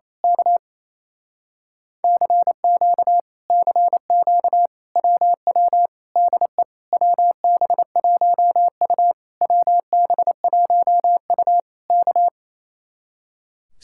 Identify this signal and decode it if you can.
Morse code, heard as a keyed tone, about 26 wpm, 700 Hz. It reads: K CQ CQ WW DE WB1U WB1U K
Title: Speed-Racing Common QSO Phrases Encore - 28wpm